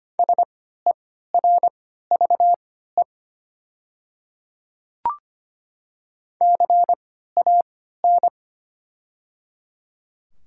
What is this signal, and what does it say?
Morse code, heard as a keyed tone, about 25 wpm, spaced as for 14 wpm, 700 Hz. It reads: SERVE CAN
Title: 3x Character Spacing - Top 500 Words - 25wpm